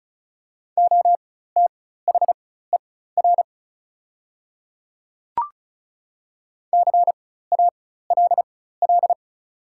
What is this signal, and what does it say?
Morse code, heard as a keyed tone, about 36 wpm, spaced as for 16 wpm, 700 Hz. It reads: OTHER CALL
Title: 4x Character Spacing - Top 200 Words - 35wpm